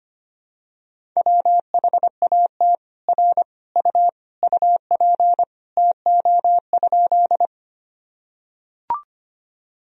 Morse code, heard as a keyed tone, about 25 wpm, 700 Hz. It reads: WHAT R U UP TO?
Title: Sentences from Top 100 Words and Abbreviations 25wpm